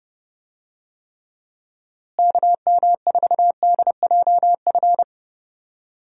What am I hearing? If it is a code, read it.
Morse code, heard as a keyed tone, about 30 wpm, 700 Hz. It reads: KM4DJF
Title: USA Call Signs - 6 Characters - Repeated 2x - 30wpm